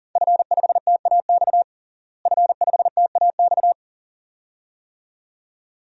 Morse code, heard as a keyed tone, about 40 wpm, 690 Hz. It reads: F5TAX F5TAX
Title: DXCC-100 Call Signs - 5 Characters - Repeated 2x - 40wpm